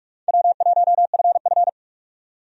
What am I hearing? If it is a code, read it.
Morse code, heard as a keyed tone, about 46 wpm, 700 Hz. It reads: W1FF
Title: USA Call Signs - 4 Characters - 45wpm